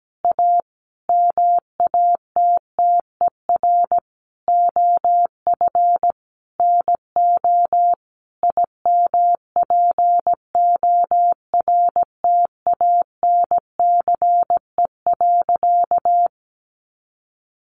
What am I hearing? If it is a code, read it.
Morse code, heard as a keyed tone, about 17 wpm, 700 Hz. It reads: A MATTER OF NO IMPORTANCE.
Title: Sentences from Top 1200 Words 17wpm